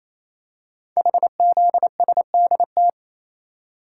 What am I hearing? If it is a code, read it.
Morse code, heard as a keyed tone, about 27 wpm, 700 Hz. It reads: HZSDT
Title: Code Groups - 5 Letters - 28wpm